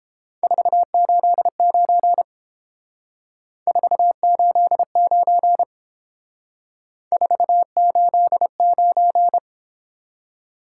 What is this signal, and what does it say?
Morse code, heard as a keyed tone, about 30 wpm, 700 Hz. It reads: 489 489 489